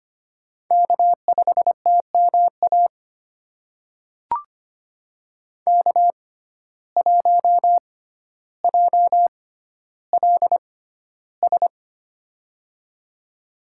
Morse code, heard as a keyed tone, about 25 wpm, spaced as for 13 wpm, 700 Hz. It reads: K5TMA K1JLS
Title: USA Call Signs - 5 Characters - 6x Character Spacing - 25wpm